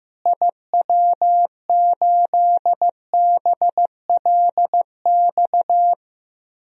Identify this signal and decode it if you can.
Morse code, heard as a keyed tone, about 15 wpm, 700 Hz. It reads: IW8BLX
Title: DXCC-100 Call Signs - 6 Characters - Repeated 3x - 15wpm